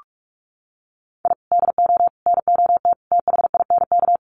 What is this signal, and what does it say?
Morse code, heard as a keyed tone, about 44 wpm, 700 Hz. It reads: I DO NOT THINK
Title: Sets of 5 Words 45wpm